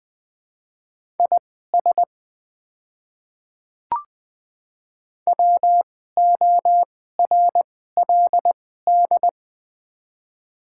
Morse code, heard as a keyed tone, about 20 wpm, spaced as for 14 wpm, 700 Hz. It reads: IS WORLD
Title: 2x Character Spacing - Top 300 Words - 20wpm